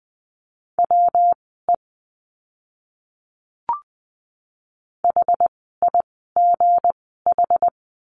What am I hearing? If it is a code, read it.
Morse code, heard as a keyed tone, about 20 wpm, spaced as for 14 wpm, 700 Hz. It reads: WE HIGH
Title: 237 Lesson 38f - Letter q - Words 20wpm